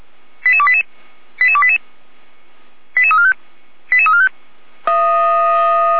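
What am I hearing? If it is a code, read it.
Five-tone selective call (ZVEI), heard as quick runs of five tones, tones of about 70 ms. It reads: 88288 88288 88345 88345
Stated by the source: ZVEI five-tone selective call received by SDR.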